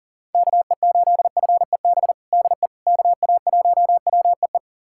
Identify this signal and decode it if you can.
Morse code, heard as a keyed tone, about 40 wpm, 700 Hz. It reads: KE8FEB DE KA1WEE
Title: USA Callsign DE Callsign - 6 Characters - 40wpm